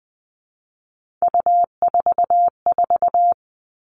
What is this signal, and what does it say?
Morse code, heard as a keyed tone, about 20 wpm, 700 Hz. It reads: U44
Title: All Cut Numbers - 3-Digit 20wpm